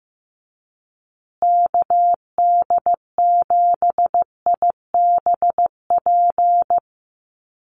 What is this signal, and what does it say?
Morse code, heard as a keyed tone, about 15 wpm, 700 Hz. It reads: KD7IBP
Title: USA Call Signs Repeated 3x 15wpm